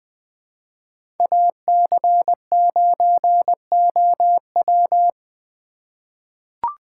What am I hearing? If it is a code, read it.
Morse code, heard as a keyed tone, about 20 wpm, 700 Hz. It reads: AC9OW
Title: USA Call Signs - 5 Characters - 4x Character Spacing - 20wpm